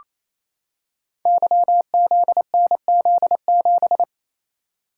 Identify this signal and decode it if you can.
Morse code, heard as a keyed tone, about 29 wpm, 700 Hz. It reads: YZNZ7
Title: Code Groups - 5 Letter:Numbers - 28wpm